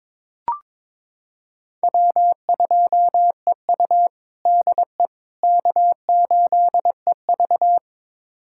Morse code, heard as a keyed tone, about 22 wpm, 700 Hz. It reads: W2EU DE K8EV
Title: USA Callsign DE Callsign - 4 Characters - 22wpm